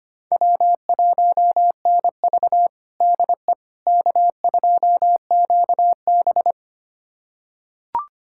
Morse code, heard as a keyed tone, about 25 wpm, 700 Hz. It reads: W1NV DE K2QB
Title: USA Callsign DE Callsign - 4 Characters - 25wpm